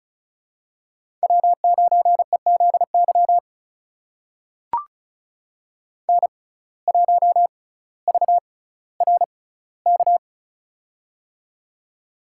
Morse code, heard as a keyed tone, about 36 wpm, spaced as for 18 wpm, 700 Hz. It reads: W9EZY N1VRK
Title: USA Call Signs - 5 Characters - 6x Character Spacing - 35wpm